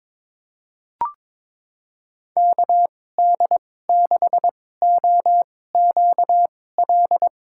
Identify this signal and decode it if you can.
Morse code, heard as a keyed tone, about 21 wpm, spaced as for 16 wpm, 700 Hz. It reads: KD6OQL
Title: USA Call Signs - 2x Character Spacing - 22wpm